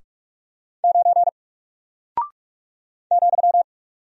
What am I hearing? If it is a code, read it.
Morse code, heard as a keyed tone, about 46 wpm, 700 Hz. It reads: 9 ,
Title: Single Character (Letter:Number:Punctuation:Prosign) Alt 45wpm